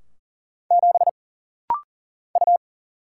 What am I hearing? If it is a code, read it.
Morse code, heard as a keyed tone, about 40 wpm, 700 Hz. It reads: 7 U
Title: Single Letter-Number - Rapid-Fire 40wpm